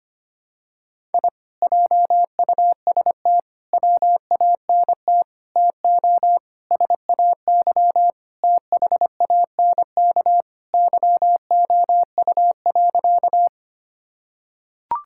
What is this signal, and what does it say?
Morse code, heard as a keyed tone, about 25 wpm, 700 Hz. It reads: I JUST WANT TO SAY THANK YOU.
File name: practice / 25wpm / Sentences with 2-Word Phrases - 25wpm